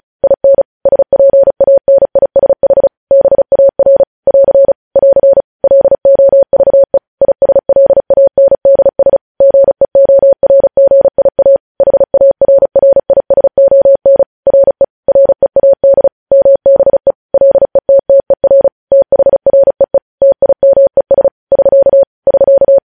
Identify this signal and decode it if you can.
Morse code, heard as a keyed tone, about 36 wpm, 550 Hz. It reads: IN SPANISH BAR <AR> <AR> LOVE ISLANDS GEORGIA HARRISON RE READ MBE LETTER THREE TIMES <SK> <SK>
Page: news headlines in morse code at 35 wpm